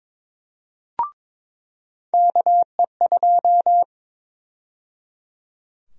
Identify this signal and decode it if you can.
Morse code, heard as a keyed tone, about 22 wpm, 700 Hz. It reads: KE2
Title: USA Call Sign Prefixes - 22wpm